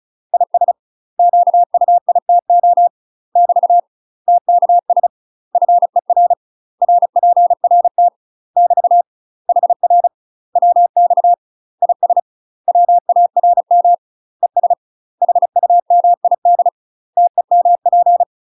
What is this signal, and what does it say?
Morse code, heard as a keyed tone, about 34 wpm, 700 Hz. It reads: IS QUITO <BT> TKS FER RPRT <BT> HR WX IS WARM ES HUMID TEMP
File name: UK MASTER QSOs TRAINING / 35WPM / UK MASTER QSOs 0030